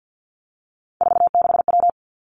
Morse code, heard as a keyed tone, about 50 wpm, 700 Hz. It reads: 46W
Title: Three Challenging Letter-Number Combinations 50wpm